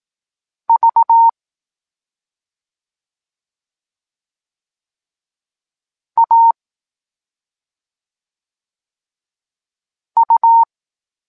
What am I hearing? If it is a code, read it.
Morse code, heard as a keyed tone, about 18 wpm, 920 Hz. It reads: V A U
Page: Tekst 14 – doskonalenie odbioru poznanych liter